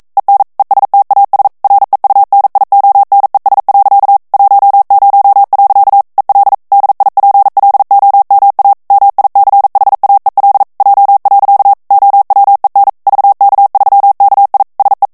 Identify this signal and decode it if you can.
Morse code, heard as a keyed tone, about 42 wpm, 800 Hz. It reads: EN ESTAS REUNIONES. 10. EL DIPLOMA MICHAEL J. OWEN VK3KI SE
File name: morse mp3 / ESPAGNOL TEXTE EN CLAIR 1 VITESSE 40 MOTS MINUTES